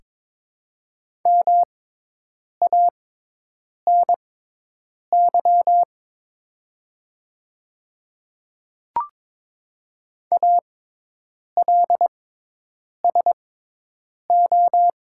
Morse code, heard as a keyed tone, about 22 wpm, spaced as for 8 wpm, 700 Hz. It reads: MANY ALSO
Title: 6x Character Spacing - Top 300 Words - 22wpm